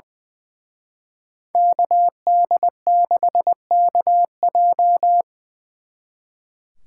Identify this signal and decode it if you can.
Morse code, heard as a keyed tone, about 25 wpm, 700 Hz. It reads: KD6KJ
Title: Speed Racing - USA 2x2 Call Signs 20wpm